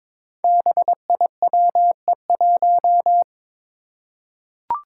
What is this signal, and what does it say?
Morse code, heard as a keyed tone, about 21 wpm, 700 Hz. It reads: BIWE1
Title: Five Challenging Letter-Number Combinations 22wpm